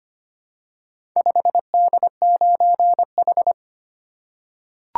Morse code, heard as a keyed tone, about 25 wpm, 700 Hz. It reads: 5D9H
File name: practice / 25wpm / Four Letter-Number Combinations - 25wpm